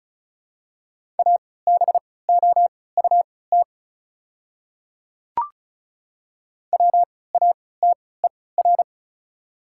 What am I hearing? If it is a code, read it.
Morse code, heard as a keyed tone, about 36 wpm, spaced as for 20 wpm, 700 Hz. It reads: ABOUT WATER
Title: 3x Character Spacing - Top 100 Words - 35wpm